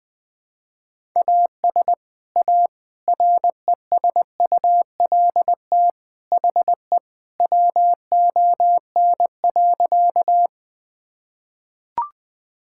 Morse code, heard as a keyed tone, about 20 wpm, 700 Hz. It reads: AS A RESULT HE WON.
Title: Sentences with 3-Word Phrases - Spoken Beginning - 20wpm